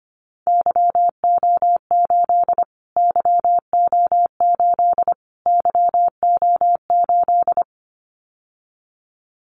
Morse code, heard as a keyed tone, about 25 wpm, 700 Hz. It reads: YO8 YO8 YO8
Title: Callsign Prefixes for Top 25 DXCC - Repeated 3x - 25wpm